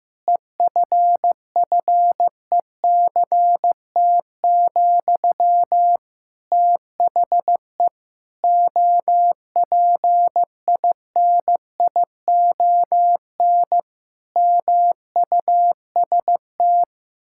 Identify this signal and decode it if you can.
Morse code, heard as a keyed tone, about 16 wpm, 700 Hz. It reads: EFFECT, THE OPINION MUST
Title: Sentences from Top 1600 Words 15wpm